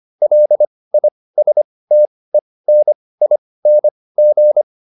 Morse code, heard as a keyed tone, about 25 wpm, spaced as for 18 wpm, 600 Hz. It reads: LISTENING